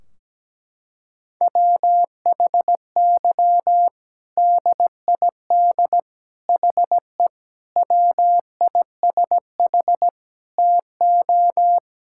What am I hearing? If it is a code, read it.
Morse code, heard as a keyed tone, about 17 wpm, 700 Hz. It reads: WHY DID HE WISH TO
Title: Sentences from Top 1600 Words 17wpm